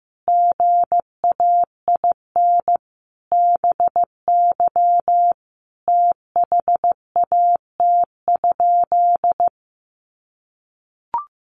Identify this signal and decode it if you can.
Morse code, heard as a keyed tone, about 15 wpm, 700 Hz. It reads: GAIN BY THAT?
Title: Sentences from Top 1100 Words 15wpm